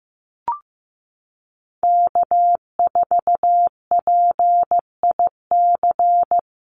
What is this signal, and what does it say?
Morse code, heard as a keyed tone, about 15 wpm, 700 Hz. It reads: K4PIC
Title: NA SOTA Chaser Callsigns 15wpm